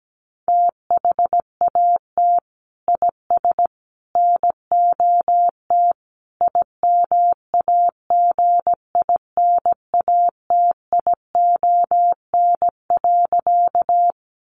Morse code, heard as a keyed tone, about 17 wpm, 700 Hz. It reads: THAT IS NOT IMAGINATION.
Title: Sentences from Top 1300 Words 17wpm